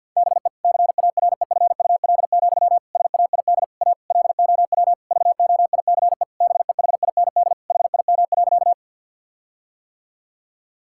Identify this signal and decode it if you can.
Morse code, heard as a keyed tone, about 50 wpm, 700 Hz. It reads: BE CAREFUL, SAID A LOW VOICE BEHIND HIM.
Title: Sentences from Top 1600 Words 50wpm